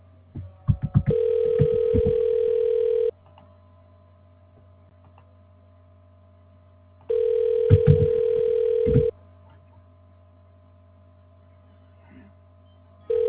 Ringback tone (North America), 440 + 480 Hz.
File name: recordings / RE58ee9ab11fcc19b81ba2ee9bb436cc47